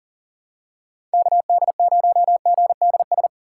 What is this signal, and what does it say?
Morse code, heard as a keyed tone, about 37 wpm, 700 Hz. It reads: KD0GDS
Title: USA Call Signs - 6 Characters - 3x Character Spacing - 40wpm